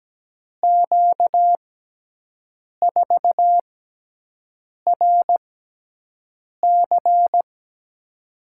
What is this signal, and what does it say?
Morse code, heard as a keyed tone, about 17 wpm, spaced as for 6 wpm, 700 Hz. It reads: Q4RC
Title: USA Call Signs - 5 Characters - 6x Character Spacing - 17wpm